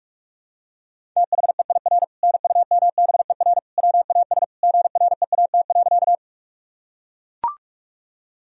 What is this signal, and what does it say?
Morse code, heard as a keyed tone, about 41 wpm, 700 Hz. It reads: THEIR NUMBER WAS GREAT.
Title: Sentences from Top 1000 Words - Review Part 1 - 45wpm